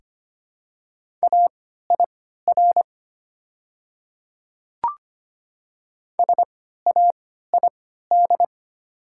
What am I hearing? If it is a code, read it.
Morse code, heard as a keyed tone, about 25 wpm, spaced as for 14 wpm, 690 Hz. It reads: AIR SAID
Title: 3x Character Spacing - Top 200 Words - 25wpm